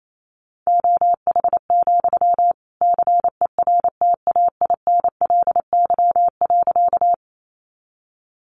Morse code, heard as a keyed tone, about 28 wpm, 700 Hz. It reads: OH, CERTAINLY.
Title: Sentences from Top 900 Words 28wpm